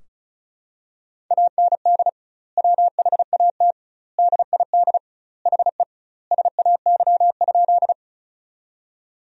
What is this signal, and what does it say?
Morse code, heard as a keyed tone, about 34 wpm, 700 Hz. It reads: AND WHAT DID HE SAY?
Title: 2x Word-Spacing - Sentences from Top 200 Words - 35wpm